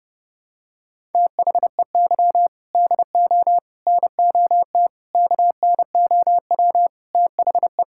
Morse code, heard as a keyed tone, about 32 wpm, 700 Hz. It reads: THEY DO NOT KNOW THE
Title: Sentences from Top 1000 Words - Review Part 6 - 30wpm